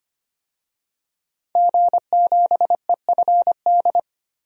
Morse code, heard as a keyed tone, about 25 wpm, 700 Hz. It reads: G7EFD
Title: DXCC-100 Call Signs - 5 Characters - 25wpm